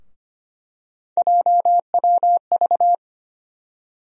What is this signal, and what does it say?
Morse code, heard as a keyed tone, about 25 wpm, 700 Hz. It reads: JWV
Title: USA Call Sign Postfixes - 3 Letters 25wpm